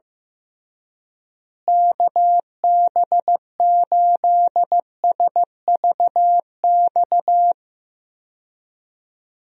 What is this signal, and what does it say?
Morse code, heard as a keyed tone, about 15 wpm, 700 Hz. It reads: KB8SVX
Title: USA Call Signs - 6 Characters - Repeated 2x - 15wpm